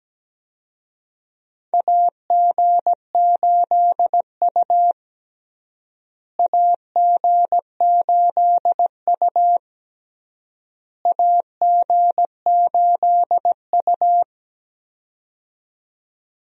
Morse code, heard as a keyed tone, about 17 wpm, 700 Hz. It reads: AG8U AG8U AG8U
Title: USA Call Signs - 4 Characters - Repeated 3x - 17wpm